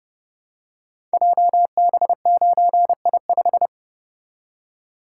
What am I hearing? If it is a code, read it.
Morse code, heard as a keyed tone, about 30 wpm, 700 Hz. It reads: JB9I5
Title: Code Groups - 5 Letter:Numbers - 30wpm